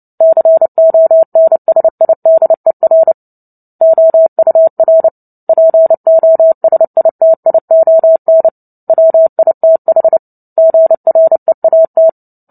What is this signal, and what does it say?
Morse code, heard as a keyed tone, about 30 wpm, 650 Hz. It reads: CONSIDER OUR POSITION WITH GREAT
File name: Book Two / 03-Chapter 3 - 29 WPM